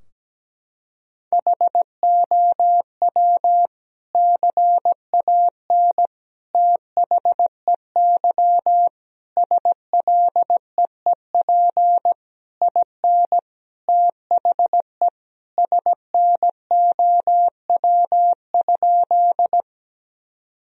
Morse code, heard as a keyed tone, about 17 wpm, 700 Hz. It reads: HOW CAN THEY SLEEP IN THE SNOW?
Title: Sentences from Top 1000 Words - Review Part 3 - 17wpm